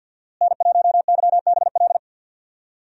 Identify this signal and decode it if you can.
Morse code, heard as a keyed tone, about 52 wpm, 700 Hz. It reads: N1YBL
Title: USA Call Signs - 5 Characters - Repeated 2x - 50wpm